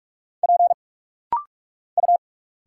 Morse code, heard as a keyed tone, about 46 wpm, 700 Hz. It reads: P U
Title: Single Letters - Rapid-Fire 45wpm